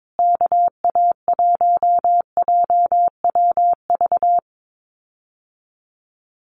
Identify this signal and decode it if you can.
Morse code, heard as a keyed tone, about 22 wpm, 700 Hz. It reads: KA1JWV
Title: Common QSO Phrases Encore 22wpm